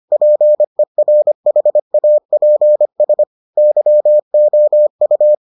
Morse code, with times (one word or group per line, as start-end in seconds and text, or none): 0.1-3.2 PERHAPS
3.6-5.3 YOU